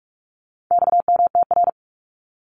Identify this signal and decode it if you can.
Morse code, heard as a keyed tone, about 44 wpm, 700 Hz. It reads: XMTR
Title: Abbreviations Repeated 3x - 45wpm